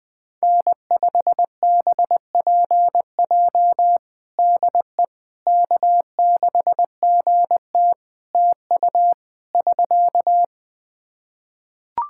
Morse code, heard as a keyed tone, about 20 wpm, 700 Hz. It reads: N5BPJ DE K6GT TU <SK>
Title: K1USN SST QSO Practice 20wpm